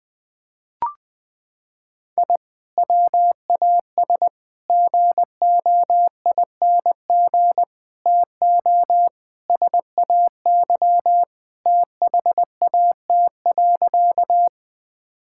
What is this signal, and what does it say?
Morse code, heard as a keyed tone, about 21 wpm, 700 Hz. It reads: I WAS GOING TO SAY THAT.
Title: Sentences with 5-Word Phrases - 20wpm